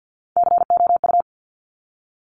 Morse code, heard as a keyed tone, about 50 wpm, 700 Hz. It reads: COU
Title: Top Three Letter Combinations in Words 50wpm